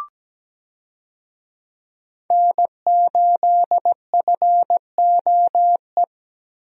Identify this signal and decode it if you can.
Morse code, heard as a keyed tone, about 17 wpm, 700 Hz. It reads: N8FOE
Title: USA Call Signs - 5 Characters - Repeated 2x - 17wpm